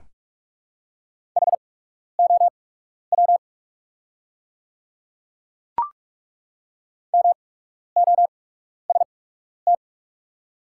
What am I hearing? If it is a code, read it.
Morse code, heard as a keyed tone, about 43 wpm, spaced as for 12 wpm, 690 Hz. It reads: HOW MOST
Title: 8x Character Spacing - Top 200 Words - 45wpm